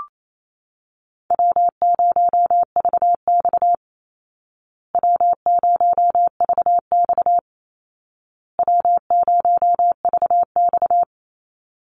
Morse code, heard as a keyed tone, about 28 wpm, 700 Hz. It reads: W0VX W0VX W0VX
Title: Call Signs Repeated 2x - 28wpm